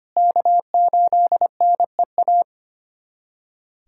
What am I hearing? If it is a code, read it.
Morse code, heard as a keyed tone, about 25 wpm, 700 Hz. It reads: K8NEA
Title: USA Callsign DE Callsign 25wpm